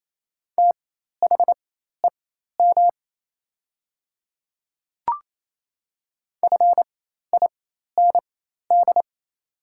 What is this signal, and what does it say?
Morse code, heard as a keyed tone, about 27 wpm, spaced as for 13 wpm, 700 Hz. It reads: THEM FIND
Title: 4x Character Spacing - Top 200 Words - 28wpm